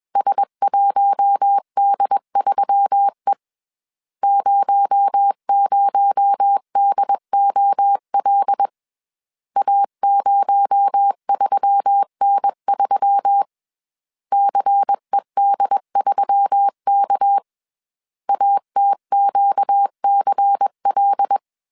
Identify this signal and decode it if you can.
Morse code, heard as a keyed tone, about 21 wpm, 800 Hz. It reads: S1D3E 00DOL A03N3 CED3K ATQCL
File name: CW / DET / 1 6 1500C S C 20wpm 2116